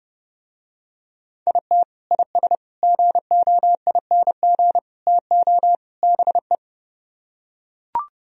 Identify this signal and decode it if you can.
Morse code, heard as a keyed tone, about 30 wpm, 700 Hz. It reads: IT IS GOING TO BE